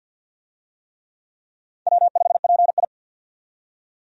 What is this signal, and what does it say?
Morse code, heard as a keyed tone, about 50 wpm, 700 Hz. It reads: W5PI
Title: USA Call Signs - 4 Characters - 50wpm